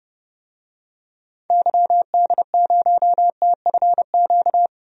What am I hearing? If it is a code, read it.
Morse code, heard as a keyed tone, about 32 wpm, 700 Hz. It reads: YD0TFQ
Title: DXCC-100 Call Signs - 6 Characters - Repeated 3x - 30wpm